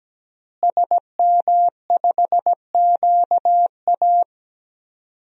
Morse code, heard as a keyed tone, about 17 wpm, 700 Hz. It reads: SM5QA
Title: DXCC-100 Call Signs - 5 Characters - Repeated 2x - 17wpm